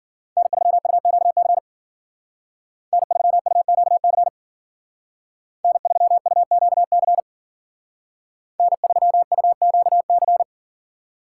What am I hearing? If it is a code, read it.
Morse code, heard as a keyed tone, about 52 wpm, 700 Hz. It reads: N3UQC N3UQC N3UQC N3UQC